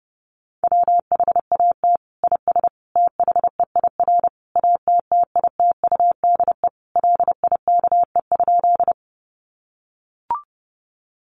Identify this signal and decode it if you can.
Morse code, heard as a keyed tone, about 30 wpm, 700 Hz. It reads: WHAT IS THEIR ATTITUDE LIKE?
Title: Sentences from Top 1300 Words 30wpm